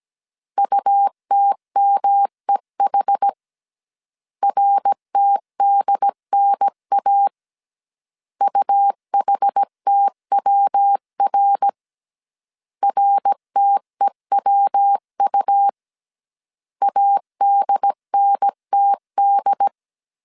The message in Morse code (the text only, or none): UTMEH RTDNA UHTWR RTEWU ADNTD